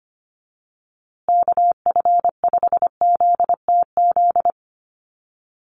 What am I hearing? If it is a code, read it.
Morse code, heard as a keyed tone, about 25 wpm, 700 Hz. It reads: KF5ZTZ
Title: USA Call Signs Repeated 3x 25wpm